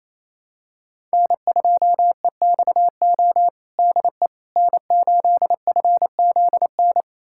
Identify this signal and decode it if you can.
Morse code, heard as a keyed tone, about 28 wpm, 700 Hz. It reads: N2EXO DE N8FZN
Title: USA Callsign DE Callsign - 5 Characters - 28wpm